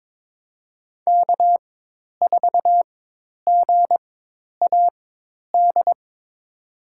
Morse code, heard as a keyed tone, about 22 wpm, spaced as for 10 wpm, 700 Hz. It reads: K4GAD
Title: USA Call Signs - 5 Characters - 4x Character Spacing - 22wpm